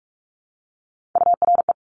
Morse code, heard as a keyed tone, about 46 wpm, 700 Hz. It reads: URE
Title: Top Three Letter Combinations in Words 45wpm